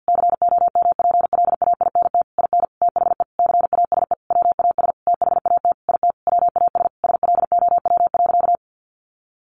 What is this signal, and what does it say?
Morse code, heard as a keyed tone, about 50 wpm, 700 Hz. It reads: COMPLAINT IN THE CASE WAS THAT IT WAS SLOW.
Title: Sentences from Top 1400 Words 50wpm